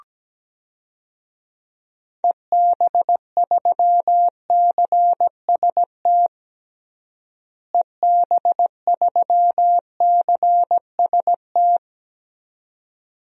Morse code, heard as a keyed tone, about 17 wpm, 700 Hz. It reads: EB3CST EB3CST
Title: DXCC-100 Call Signs - 6 Characters - Repeated 2x - 17wpm